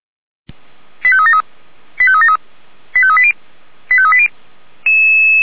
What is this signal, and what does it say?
Five-tone selective call (ZVEI), heard as quick runs of five tones, tones of about 70 ms. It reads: 85373 85373 85380 85380
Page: /464/SDR/ZVEI/85703/2025.04.22/